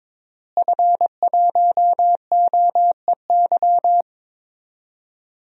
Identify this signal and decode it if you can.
Morse code, heard as a keyed tone, about 22 wpm, 700 Hz. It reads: F1OEY
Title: DXCC-100 Call Signs - 5 Characters - Repeated 3x - 22wpm